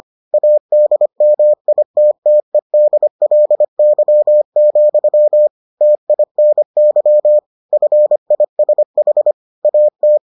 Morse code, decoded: ADMITTEDLY, TINY FISH AT

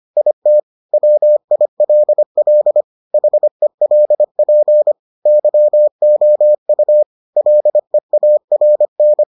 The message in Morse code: IT WILL HELP YOU LEARN

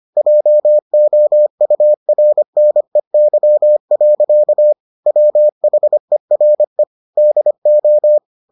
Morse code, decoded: JOURNEY. WHERE DO

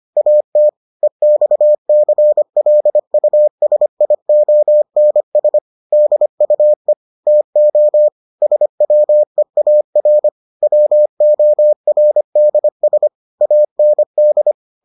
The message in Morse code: AT EXCLUSIONS DUE TO SWEAR WORDS AND

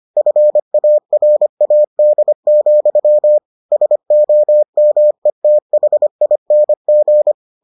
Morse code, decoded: FARAD, SOMETHING

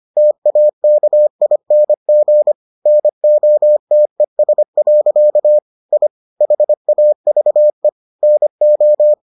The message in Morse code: TAKING NOTES. I HAVE NO

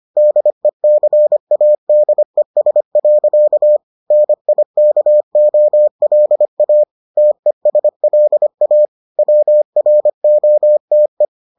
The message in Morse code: DECADES. NIKOLA TESLA WROTE

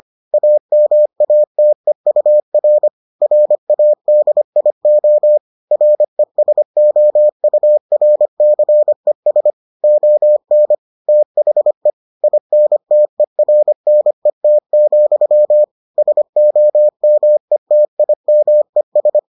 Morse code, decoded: AMATEUR RADIO RESOURCES ON THE INTERNET, SOMETIMES

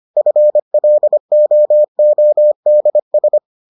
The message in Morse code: FLOODS

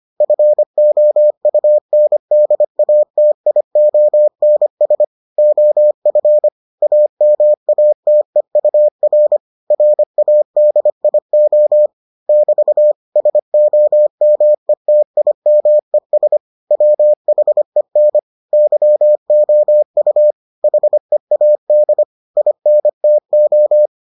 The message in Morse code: FOUNDATIONS OF AMATEUR RADIO <BT> SOMETIMES WHEN YOU HEAD INTO

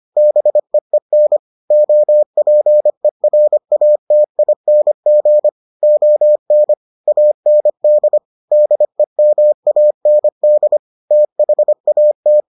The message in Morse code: BEEN OPERATING ON AND DEMAND THAT

